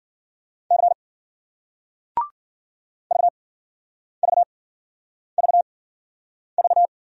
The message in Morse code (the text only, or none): C VVVV